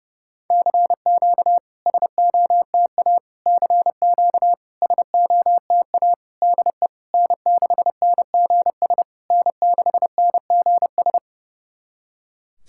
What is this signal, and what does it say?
Morse code, heard as a keyed tone, about 32 wpm, 700 Hz. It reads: CQ SOTA CQ SOTA DE N6NGS N6NGS
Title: SOTA QSO Practice 30wpm